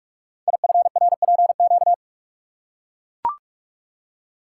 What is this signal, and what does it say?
Morse code, heard as a keyed tone, about 44 wpm, 700 Hz. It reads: IFRPQ
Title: Code Groups - 5 Letter:Numbers - 45wpm